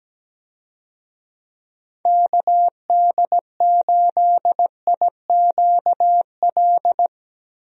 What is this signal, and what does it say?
Morse code, heard as a keyed tone, about 17 wpm, 700 Hz. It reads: KD8IQL
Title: USA Call Signs - 6 Characters - Repeated 3x - 17wpm